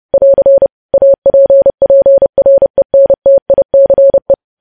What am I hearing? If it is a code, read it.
Morse code, heard as a keyed tone, about 30 wpm, 550 Hz. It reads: <AR> APPRENTICE
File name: podcast / News headlines-1x-30x30